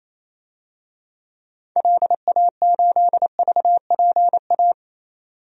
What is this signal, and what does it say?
Morse code, heard as a keyed tone, about 28 wpm, 700 Hz. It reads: LA8VPA
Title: DXCC-100 Call Signs - 6 Characters - 28wpm